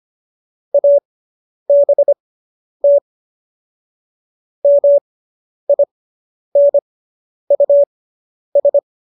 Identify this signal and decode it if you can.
Morse code, heard as a keyed tone, about 25 wpm, spaced as for 10 wpm, 570 Hz. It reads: ABT MINUS